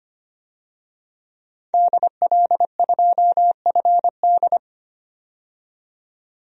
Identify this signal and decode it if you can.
Morse code, heard as a keyed tone, about 25 wpm, 700 Hz. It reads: DL2FD
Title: DXCC-100 Call Signs - 5 Characters - 25wpm